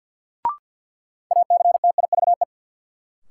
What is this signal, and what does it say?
Morse code, heard as a keyed tone, about 53 wpm, 700 Hz. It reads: ACTIVE
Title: Sentences from Top 900 Words - New Word Drill - 50wpm